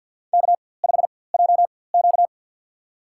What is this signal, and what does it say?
Morse code, heard as a keyed tone, about 53 wpm, spaced as for 24 wpm, 690 Hz. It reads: K5JQ
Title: USA Call Signs - 4x Character Spacing - 50wpm